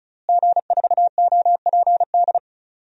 Morse code, heard as a keyed tone, about 34 wpm, 700 Hz. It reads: G4OPD
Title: DXCC-100 Call Signs - 5 Characters - Repeated 3x - 35wpm